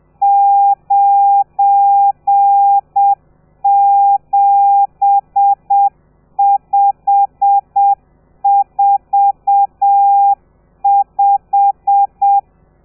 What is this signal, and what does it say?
Morse code, heard as a keyed tone, about 7 wpm, 800 Hz. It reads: 97545
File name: telegrafia / i2tfj / Lez12 30